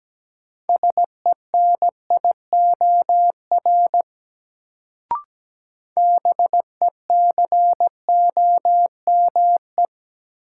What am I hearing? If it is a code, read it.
Morse code, heard as a keyed tone, about 17 wpm, 700 Hz. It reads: SENIOR BECOME
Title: Sets of 5 Words Encore - New Word Drill - 17wpm